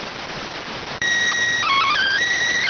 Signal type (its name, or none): SSTV (slow-scan television)